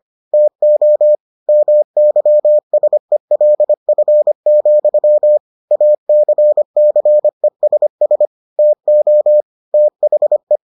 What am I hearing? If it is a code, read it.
Morse code, heard as a keyed tone, about 25 wpm, 600 Hz. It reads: TO MYSELF, ACCESS TO THE